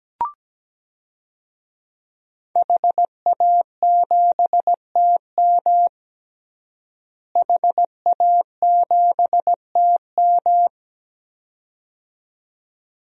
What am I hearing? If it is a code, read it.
Morse code, heard as a keyed tone, about 17 wpm, 700 Hz. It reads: HA7TM HA7TM
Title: DXCC-100 Call Signs - 5 Characters - Repeated 2x - 17wpm